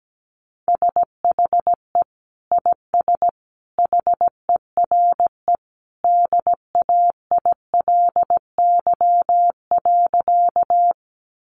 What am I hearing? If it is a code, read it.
Morse code, heard as a keyed tone, about 17 wpm, 700 Hz. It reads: SHE IS HERE DAILY.
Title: Sentences from Top 1100 Words 17wpm